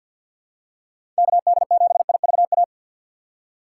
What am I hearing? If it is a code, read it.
Morse code, heard as a keyed tone, about 52 wpm, 700 Hz. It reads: KD7IVA
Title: USA Call Signs - 6 Characters - 4x Character Spacing - 50wpm